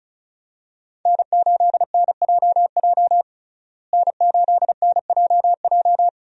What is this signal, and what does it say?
Morse code, heard as a keyed tone, about 36 wpm, 700 Hz. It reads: N8NJJ N8NJJ
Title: USA Call Signs - 5 Characters - Repeated 2x - 35wpm